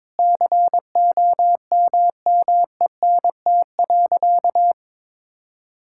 Morse code, heard as a keyed tone, about 22 wpm, 700 Hz. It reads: COMMENT.
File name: Sentences with 7-Word Phrases - 22wpm